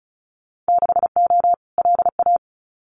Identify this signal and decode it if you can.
Morse code, heard as a keyed tone, about 34 wpm, 700 Hz. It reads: 6O LA
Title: Winter Field Day Exchanges Encore 35wpm